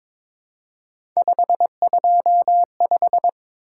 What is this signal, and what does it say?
Morse code, heard as a keyed tone, about 22 wpm, 700 Hz. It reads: 525
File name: practice / 22wpm / Three Letter-Number Combinations - 22wpm